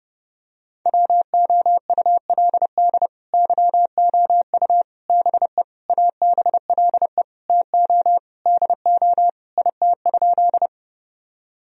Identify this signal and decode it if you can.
Morse code, heard as a keyed tone, about 30 wpm, 700 Hz. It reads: WOULD YOU BE ABLE TO DO IT?
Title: Sentences from Top 1000 Words - Review Part 2 - 30wpm